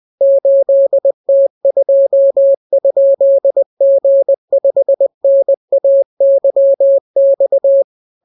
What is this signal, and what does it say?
Morse code, heard as a keyed tone, about 20 wpm, 550 Hz. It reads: 8T2?G5NAYX